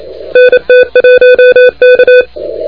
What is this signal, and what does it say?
Morse code, heard as a keyed tone, about 28 wpm, 3500 Hz, with noise. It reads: NT1K
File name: recordings / HG7T de NT1K ARRL10M CW 28MHz